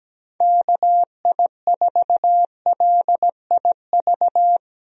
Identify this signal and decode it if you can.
Morse code, heard as a keyed tone, about 17 wpm, 700 Hz. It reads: KI4LIV
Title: USA Call Signs - 6 Characters - Repeated 3x - 17wpm